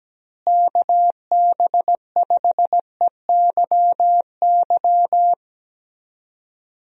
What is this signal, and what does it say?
Morse code, heard as a keyed tone, about 17 wpm, 700 Hz. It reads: KB5EYY
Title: USA Call Signs - 6 Characters - Repeated 2x - 17wpm